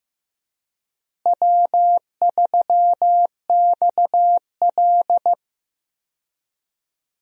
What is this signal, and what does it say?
Morse code, heard as a keyed tone, about 15 wpm, 700 Hz. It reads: W3XL
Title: USA Call Signs - 4 Characters - Repeated 3x - 15wpm